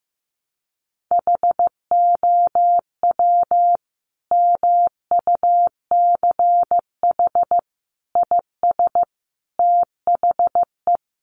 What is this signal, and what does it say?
Morse code, heard as a keyed tone, about 15 wpm, 700 Hz. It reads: HOW MUCH IS THE
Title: Sentences from Top 1200 Words 15wpm